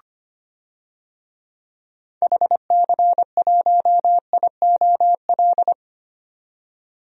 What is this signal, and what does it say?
Morse code, heard as a keyed tone, about 25 wpm, 700 Hz. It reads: HC1IOL
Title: DXCC-100 Call Signs - 6 Characters - 25wpm